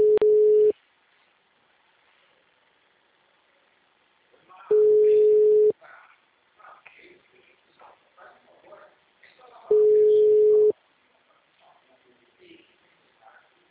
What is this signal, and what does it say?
Ringback tone (Europe, 425 Hz), with speech.
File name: out- -2013- - - .642